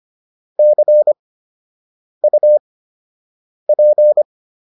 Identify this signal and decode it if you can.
Morse code, heard as a keyed tone, about 25 wpm, spaced as for 7 wpm, 600 Hz. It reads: CUP